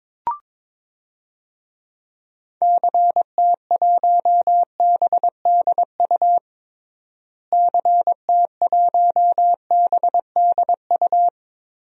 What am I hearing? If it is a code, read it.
Morse code, heard as a keyed tone, about 22 wpm, 700 Hz. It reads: CT1BDU CT1BDU
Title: DXCC-100 Call Signs - 6 Characters - Repeated 2x - 22wpm